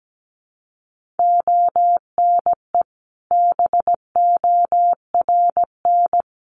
Morse code, heard as a keyed tone, about 17 wpm, 700 Hz. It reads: ONE BORN
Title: Sentences from Top 1400 Words 17wpm